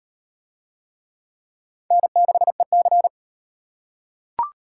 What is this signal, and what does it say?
Morse code, heard as a keyed tone, about 37 wpm, 700 Hz. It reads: N6EC